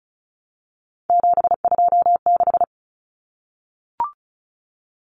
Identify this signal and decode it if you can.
Morse code, heard as a keyed tone, about 34 wpm, 700 Hz. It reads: 726
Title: Numbers - 3-Digit 35wpm